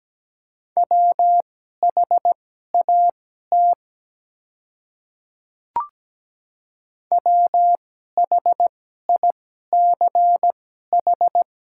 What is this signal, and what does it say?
Morse code, heard as a keyed tone, about 17 wpm, spaced as for 12 wpm, 700 Hz. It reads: WHAT WHICH
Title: 2x Character Spacing - Top 100 Words - 17wpm